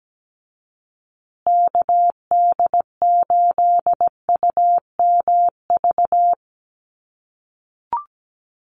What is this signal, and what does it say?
Morse code, heard as a keyed tone, about 17 wpm, 700 Hz. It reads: KD8UMV
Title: USA Call Signs Repeated 2x 17wpm